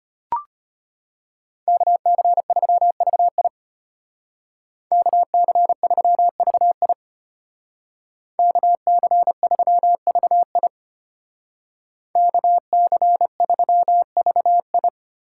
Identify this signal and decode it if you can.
Morse code, heard as a keyed tone, about 33 wpm, 700 Hz. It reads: KC3VI KC3VI KC3VI KC3VI